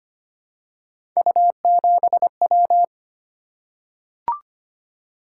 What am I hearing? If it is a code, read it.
Morse code, heard as a keyed tone, about 25 wpm, 700 Hz. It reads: U7W
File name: Three Letter-Number Combinations - 25wpm